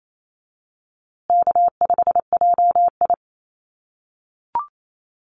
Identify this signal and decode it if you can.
Morse code, heard as a keyed tone, about 28 wpm, 700 Hz. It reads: K5JI
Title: USA Call Signs Repeated 3x 28wpm